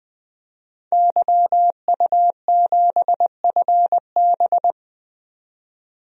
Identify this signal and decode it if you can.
Morse code, heard as a keyed tone, about 20 wpm, 700 Hz. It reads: YU7FB
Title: DXCC-100 Call Signs - 5 Characters - 20wpm